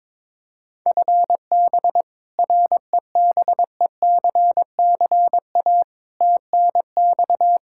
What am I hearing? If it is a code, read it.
Morse code, heard as a keyed tone, about 22 wpm, 700 Hz. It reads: FB REBECCA TNX
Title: Practice QSO - Exchange 3 - More Info - 22wpm